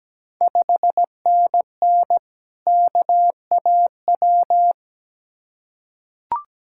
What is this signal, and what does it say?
Morse code, heard as a keyed tone, about 17 wpm, 700 Hz. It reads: 5NN KAW
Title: Ontario QSO Party Exchange 17wpm